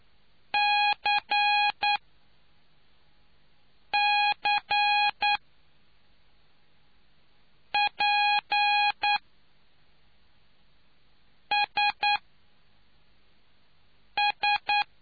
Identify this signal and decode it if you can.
Morse code, heard as a keyed tone, about 9 wpm, spaced as for 3 wpm, 830 Hz. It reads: CCPSS